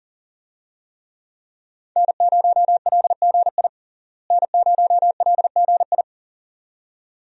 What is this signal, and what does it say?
Morse code, heard as a keyed tone, about 40 wpm, 700 Hz. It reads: N0LGI N0LGI
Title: USA Call Signs - 5 Characters - Repeated 2x - 40wpm